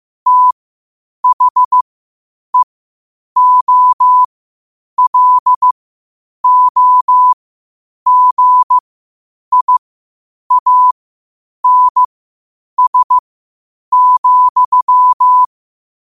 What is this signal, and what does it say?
Morse code, heard as a keyed tone, about 15 wpm, spaced as for 8 wpm, 1000 Hz. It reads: THEOLOGIANS,
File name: qotdfiles / 8wpm